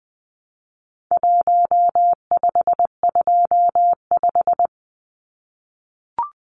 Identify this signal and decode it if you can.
Morse code, heard as a keyed tone, about 20 wpm, 700 Hz. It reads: 1525
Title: Numbers - Contest Style - 4-Digit 20wpm